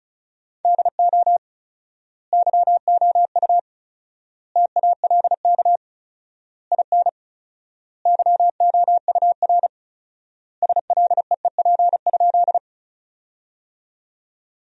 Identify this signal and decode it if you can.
Morse code, heard as a keyed tone, about 34 wpm, 700 Hz. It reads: DO YOU TALK IN YOUR SLEEP?
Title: 4x Word-Spacing - Sentences from Top 500 Words - 35wpm